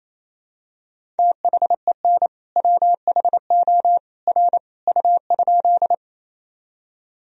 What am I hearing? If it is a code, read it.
Morse code, heard as a keyed tone, about 28 wpm, 700 Hz. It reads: THEN WHO R U?
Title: Sentences from Top 100 Words and Abbreviations 28wpm